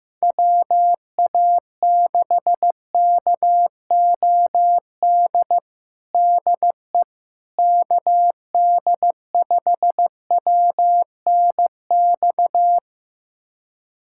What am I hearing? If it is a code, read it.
Morse code, heard as a keyed tone, about 15 wpm, 700 Hz. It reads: WA6KOD DE KD5WNX
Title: USA Callsign DE Callsign - 6 Characters - 15wpm